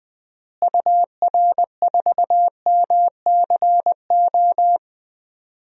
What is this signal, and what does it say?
Morse code, heard as a keyed tone, about 20 wpm, 700 Hz. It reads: UR4MCO
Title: DXCC-100 Call Signs - 6 Characters - Repeated 2x - 20wpm